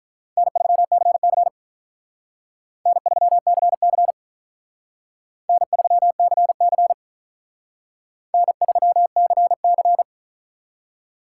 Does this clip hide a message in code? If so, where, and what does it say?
Yes: Morse code from 0.4 to 10.0 s, heard as a keyed tone, about 45 wpm, 700 Hz. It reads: N3CC N3CC N3CC N3CC